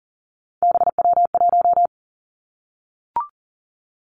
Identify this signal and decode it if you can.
Morse code, heard as a keyed tone, about 38 wpm, 700 Hz. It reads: BW1
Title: Three Challenging Letter-Number Combinations 40wpm